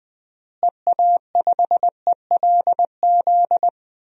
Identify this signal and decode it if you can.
Morse code, heard as a keyed tone, about 20 wpm, 700 Hz. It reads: EA5ELZ
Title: DXCC-100 Call Signs - 6 Characters - Repeated 3x - 20wpm